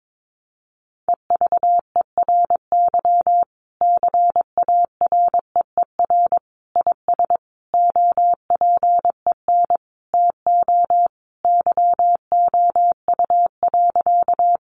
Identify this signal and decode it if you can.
Morse code, heard as a keyed tone, about 22 wpm, 700 Hz. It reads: EVERY CAREER IS OPEN TO YOU.
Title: Sentences from Top 1000 Words 22wpm